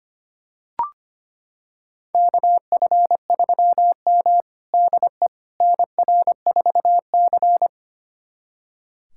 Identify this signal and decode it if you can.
Morse code, heard as a keyed tone, about 25 wpm, 700 Hz. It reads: KF3M DE NR4C
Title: USA Callsign DE Callsign - 4 Characters - 25wpm